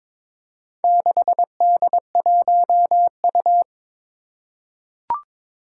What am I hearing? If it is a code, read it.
Morse code, heard as a keyed tone, about 22 wpm, 700 Hz. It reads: 6D1U
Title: Four Challenging Letter-Number Combinations 22wpm